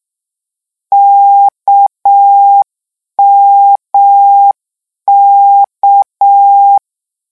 Morse code, heard as a keyed tone, about 6 wpm, 800 Hz. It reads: KMK
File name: morse mp3 / lecon 1-vitesse 6